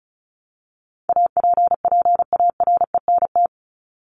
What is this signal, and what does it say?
Morse code, heard as a keyed tone, about 34 wpm, 700 Hz. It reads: APPARENT
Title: LY Suffix - Top 100 Words with Base Words 35wpm